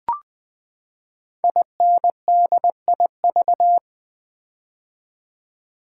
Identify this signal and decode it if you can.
Morse code, heard as a keyed tone, about 20 wpm, 700 Hz. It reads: INDIV
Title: Build A Word - 5 Letters At A Time - 10-Letter Words - 20wpm